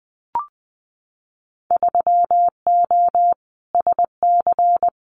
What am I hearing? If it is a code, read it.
Morse code, heard as a keyed tone, about 20 wpm, 700 Hz. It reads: 3O SC
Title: Winter Field Day Exchanges Encore 20wpm